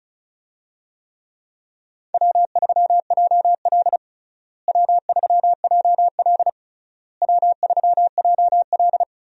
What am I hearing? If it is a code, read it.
Morse code, heard as a keyed tone, about 36 wpm, 700 Hz. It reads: W3JL W3JL W3JL
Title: USA Call Signs - 4 Characters - Repeated 3x - 35wpm